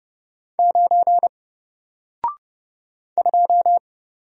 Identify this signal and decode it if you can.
Morse code, heard as a keyed tone, about 30 wpm, 700 Hz. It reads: 9 2
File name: Single-Letter-Number-30wpm